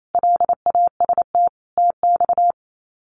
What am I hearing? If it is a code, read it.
Morse code, heard as a keyed tone, about 28 wpm, 700 Hz. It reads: LAST TX
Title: Common QSO Phrases 28wpm